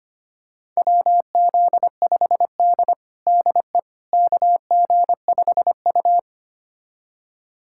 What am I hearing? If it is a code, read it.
Morse code, heard as a keyed tone, about 25 wpm, 700 Hz. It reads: WZ5D DE KG5U
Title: USA Callsign DE Callsign - 4 Characters - 25wpm